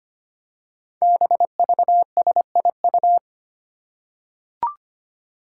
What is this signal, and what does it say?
Morse code, heard as a keyed tone, about 25 wpm, 700 Hz. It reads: BVSIU
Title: Five Challenging Letter-Number Combinations 25wpm